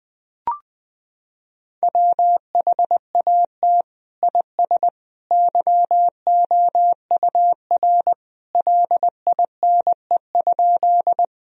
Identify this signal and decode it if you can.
Morse code, heard as a keyed tone, about 21 wpm, 700 Hz. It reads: WHAT IS YOUR LINE?
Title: Sentences from Top 1000 Words - Review Part 1 - 20wpm